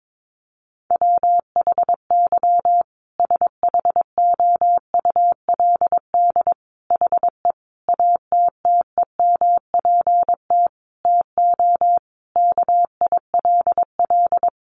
Morse code, heard as a keyed tone, about 22 wpm, 700 Hz. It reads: WHY SHOULD HE ATTEMPT TO KILL
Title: Sentences from Top 1400 Words 22wpm